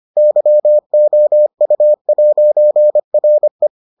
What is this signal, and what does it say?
Morse code, heard as a keyed tone, about 25 wpm, 600 Hz. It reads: YOU'RE